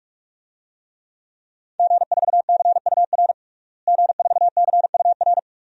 Morse code, heard as a keyed tone, about 46 wpm, 700 Hz. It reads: G4CUR G4CUR
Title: DXCC-100 Call Signs - 5 Characters - Repeated 2x - 45wpm